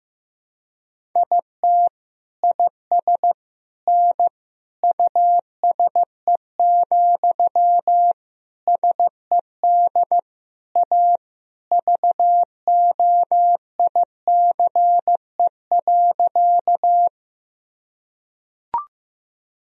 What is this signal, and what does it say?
Morse code, heard as a keyed tone, about 16 wpm, 700 Hz. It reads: IT IS N USE, SED A VOICE.
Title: Sentences from Top 500 Words and Abbreviations 15wpm